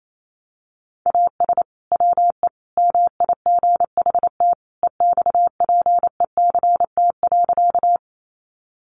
Morse code, heard as a keyed tone, about 28 wpm, 700 Hz. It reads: AS WE MIGHT EXPECT.
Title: Sentences from Top 1100 Words 28wpm